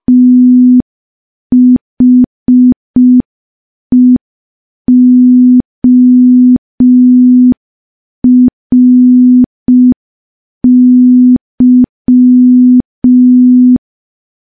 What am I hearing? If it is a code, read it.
Morse code, heard as a keyed tone, about 5 wpm, 250 Hz. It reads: THEORY
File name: morse / morse-training-05WPM